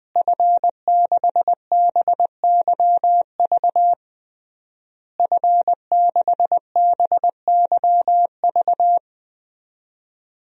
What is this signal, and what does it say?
Morse code, heard as a keyed tone, about 21 wpm, 700 Hz. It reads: F6BYV F6BYV
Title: DXCC-100 Call Signs - 5 Characters - Repeated 3x - 20wpm